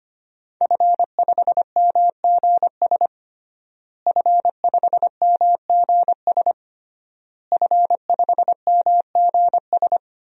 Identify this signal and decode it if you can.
Morse code, heard as a keyed tone, about 25 wpm, 700 Hz. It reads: F5MGS F5MGS F5MGS
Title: DXCC-100 Call Signs - 5 Characters - Repeated 3x - 25wpm